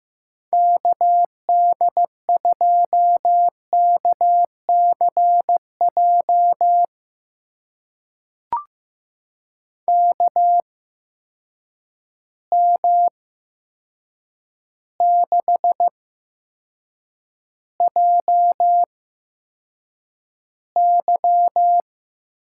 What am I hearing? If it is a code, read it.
Morse code, heard as a keyed tone, about 15 wpm, spaced as for 7 wpm, 700 Hz. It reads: KD2KCJ KM6JY
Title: USA Call Signs - 6 Characters - 8x Character Spacing - 15wpm